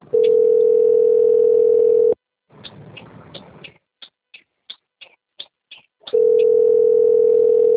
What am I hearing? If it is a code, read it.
Ringback tone (North America), 440 + 480 Hz.
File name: out- - - - - .13758